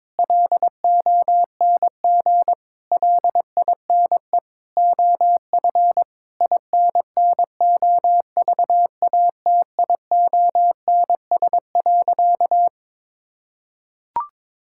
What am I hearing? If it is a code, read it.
Morse code, heard as a keyed tone, about 22 wpm, 700 Hz. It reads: LONG LINE OF INNOVATIONS.
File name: practice / 22wpm / Sentences with 8-Word Phrases - 22wpm